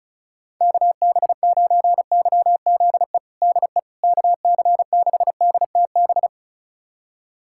Morse code, heard as a keyed tone, about 36 wpm, 690 Hz. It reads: KB9YZE DE KC6DTB
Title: USA Callsign DE Callsign - 6 Characters - 35wpm